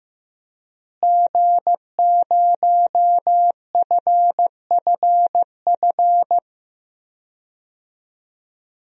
Morse code, heard as a keyed tone, about 15 wpm, 700 Hz. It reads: G0FFF
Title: DXCC-100 Call Signs - 5 Characters - Repeated 3x - 15wpm